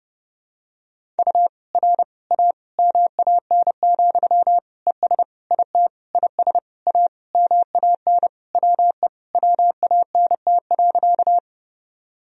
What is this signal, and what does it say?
Morse code, heard as a keyed tone, about 32 wpm, 700 Hz. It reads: U R A MAN, ES IT IS A MAN WE WANT.
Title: Sentences from Top 200 Words and Abbreviations 30wpm